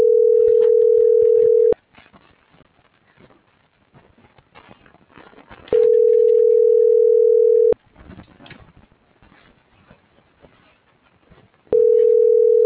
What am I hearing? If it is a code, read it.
Ringback tone (North America), 440 + 480 Hz.